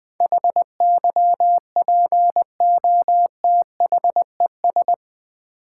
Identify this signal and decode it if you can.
Morse code, heard as a keyed tone, about 20 wpm, 700 Hz. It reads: HYPOTHES
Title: Build A Word - 4 Letters At A Time - 10-Letter Words - 20wpm